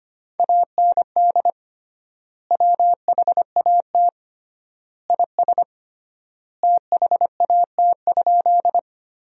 Morse code, heard as a keyed tone, about 25 wpm, 700 Hz. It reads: AND WHAT IS THAT?
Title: 3x Word-Spacing - Sentences from Top 100 Words - 25wpm